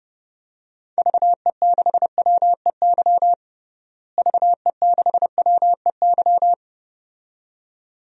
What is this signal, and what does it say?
Morse code, heard as a keyed tone, about 30 wpm, 700 Hz. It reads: VE6WEY VE6WEY
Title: DXCC-100 Call Signs - 6 Characters - Repeated 2x - 30wpm